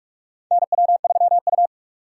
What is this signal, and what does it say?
Morse code, heard as a keyed tone, about 46 wpm, 700 Hz. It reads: NW3U
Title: USA Call Signs - 4 Characters - 6x Character Spacing - 45wpm